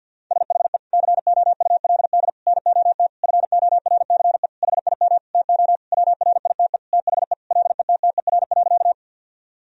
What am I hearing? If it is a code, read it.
Morse code, heard as a keyed tone, about 50 wpm, 700 Hz. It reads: SHE COULD NOT FORCE HIM TO WRITE THE LETTER.
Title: Sentences from Top 1000 Words - Review Part 3 - 50wpm